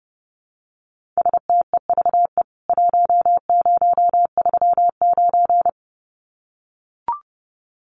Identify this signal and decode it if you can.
Morse code, heard as a keyed tone, about 30 wpm, 700 Hz. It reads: STEVE 1039
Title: CWT Exchange 30wpm